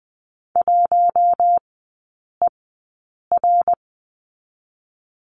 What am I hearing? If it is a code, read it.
Morse code, heard as a keyed tone, about 20 wpm, spaced as for 8 wpm, 700 Hz. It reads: 1ER
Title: 074 Lesson 13e - Letter L - Three Characters 20wpm